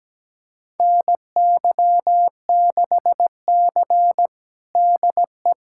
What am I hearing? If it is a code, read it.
Morse code, heard as a keyed tone, about 17 wpm, 700 Hz. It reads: NY6C DE
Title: USA Callsign DE Callsign - 4 Characters - 17wpm